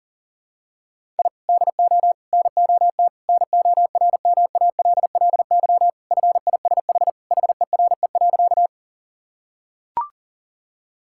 Morse code, heard as a keyed tone, about 37 wpm, 700 Hz. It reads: I DO NOT NORMALLY FISH HERE.
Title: Sentences from Top 1000 Words - Review Part 6 - 40wpm